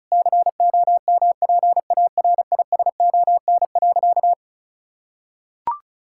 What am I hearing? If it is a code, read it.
Morse code, heard as a keyed tone, about 36 wpm, 700 Hz. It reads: COMPARISON.
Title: Sentences from Top 1600 Words 35wpm